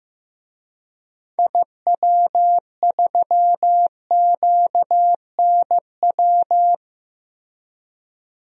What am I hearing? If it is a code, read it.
Morse code, heard as a keyed tone, about 16 wpm, 700 Hz. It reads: IW3QNW
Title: DXCC-100 Call Signs - 6 Characters - Repeated 3x - 15wpm